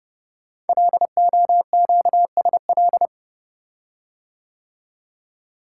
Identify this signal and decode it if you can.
Morse code, heard as a keyed tone, about 30 wpm, 700 Hz. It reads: LOQSL
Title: Code Groups - 5 Letters - 30wpm